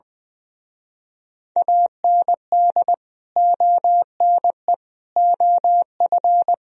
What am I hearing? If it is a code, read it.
Morse code, heard as a keyed tone, about 20 wpm, 700 Hz. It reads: AND ONE OF
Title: Speed-Racing - Sets of 5 Words 20wpm